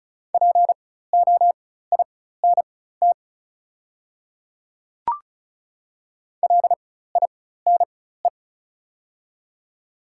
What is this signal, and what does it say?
Morse code, heard as a keyed tone, about 36 wpm, spaced as for 16 wpm, 700 Hz. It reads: POINT LINE
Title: 4x Character Spacing - Top 200 Words - 35wpm